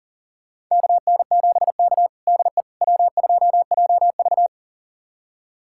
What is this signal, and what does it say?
Morse code, heard as a keyed tone, about 37 wpm, 700 Hz. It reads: KN7K DE W2JV
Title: USA Callsign DE Callsign - 4 Characters - 40wpm